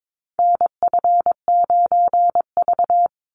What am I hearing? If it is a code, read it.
Morse code, heard as a keyed tone, about 22 wpm, 700 Hz. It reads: NF9V
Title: Call Signs - 22wpm